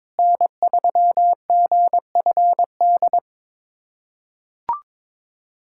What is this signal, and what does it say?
Morse code, heard as a keyed tone, about 22 wpm, 700 Hz. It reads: N3GFD
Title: USA Callsign DE Callsign - 5 Characters - 22wpm